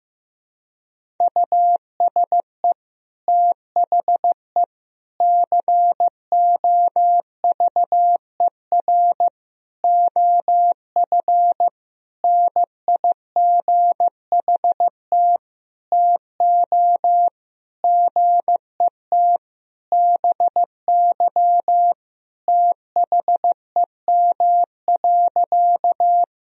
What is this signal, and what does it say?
Morse code, heard as a keyed tone, about 15 wpm, 700 Hz. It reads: USE THE COVER OF NIGHT TO GET BY THEM.
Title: Sentences from Top 1000 Words - Review Part 1 - 15wpm